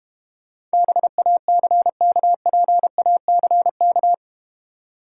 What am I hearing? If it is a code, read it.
Morse code, heard as a keyed tone, about 31 wpm, 700 Hz. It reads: BACKPACK
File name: Easy Words - Speed-Racing with Hints - 25wpm